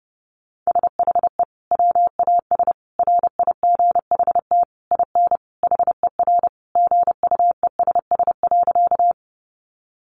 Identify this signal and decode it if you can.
Morse code, heard as a keyed tone, about 30 wpm, 700 Hz. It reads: SHE WAS RIGHT IN HER GUESS.
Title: Sentences from Top 1500 Words 30wpm